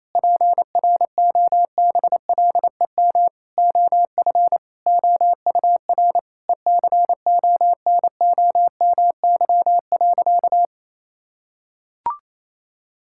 Morse code, heard as a keyed tone, about 28 wpm, 700 Hz. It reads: PROBLEM OF OUR ECONOMY.
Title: Sentences from Top 1000 Words - Review Part 5 - 28wpm